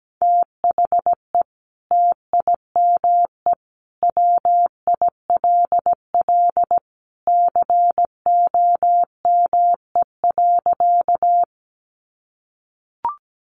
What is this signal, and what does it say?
Morse code, heard as a keyed tone, about 17 wpm, 700 Hz. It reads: THE TIME WILL COME.
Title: Sentences from Top 100 Words 17wpm